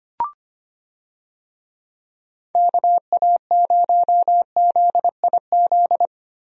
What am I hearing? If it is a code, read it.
Morse code, heard as a keyed tone, about 25 wpm, 700 Hz. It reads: KA0ZIZ
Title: USA Call Signs - 6 Characters - Repeated 3x - 25wpm